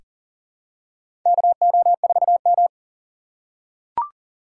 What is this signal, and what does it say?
Morse code, heard as a keyed tone, about 42 wpm, 700 Hz. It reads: KO4M
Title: USA Call Signs - 4 Characters - 8x Character Spacing - 40wpm